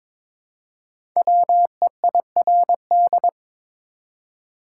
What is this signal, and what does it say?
Morse code, heard as a keyed tone, about 22 wpm, 690 Hz. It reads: WEIRD
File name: 5 Letter Words - First 3 Letters Given - 22wpm